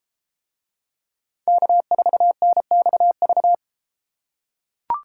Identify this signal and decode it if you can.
Morse code, heard as a keyed tone, about 31 wpm, 700 Hz. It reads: K4NXV